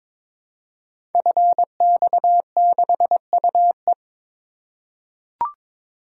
Morse code, heard as a keyed tone, about 22 wpm, 700 Hz. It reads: FX6UE
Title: Code Groups - 5 Letter:Numbers - 22wpm